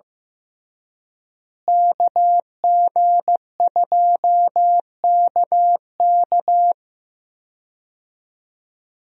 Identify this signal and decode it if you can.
Morse code, heard as a keyed tone, about 15 wpm, 700 Hz. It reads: KG2KK
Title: USA Call Signs - 5 Characters - Repeated 3x - 15wpm